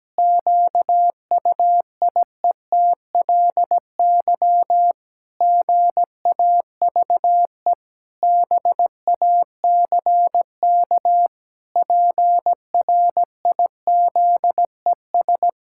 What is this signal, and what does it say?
Morse code, heard as a keyed tone, about 17 wpm, 700 Hz. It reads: QUIETLY GAVE BACK PRIZES
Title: Pangram Challenge v2 - 17wpm